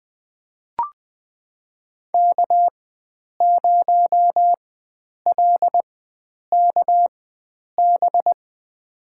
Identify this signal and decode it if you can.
Morse code, heard as a keyed tone, about 20 wpm, spaced as for 9 wpm, 700 Hz. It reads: K0LKB
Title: USA Call Signs - 4x Character Spacing - 20wpm